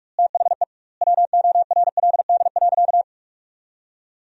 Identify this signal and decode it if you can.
Morse code, heard as a keyed tone, about 46 wpm, 700 Hz. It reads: THE WORLD.
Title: 2x Word-Spacing - Sentences from Top 200 Words - 45wpm